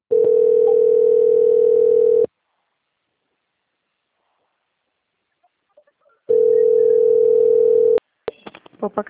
Ringback tone (North America), 440 + 480 Hz.